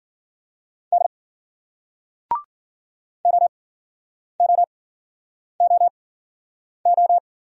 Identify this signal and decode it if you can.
Morse code, heard as a keyed tone, about 56 wpm, spaced as for 9 wpm, 700 Hz. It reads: D OOOO